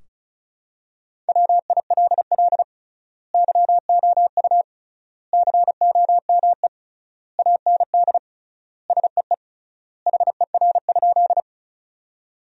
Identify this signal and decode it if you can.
Morse code, heard as a keyed tone, about 36 wpm, 700 Hz. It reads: WILL YOU COME AND SEE HER?
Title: 3x Word-Spacing - Sentences from Top 100 Words - 35wpm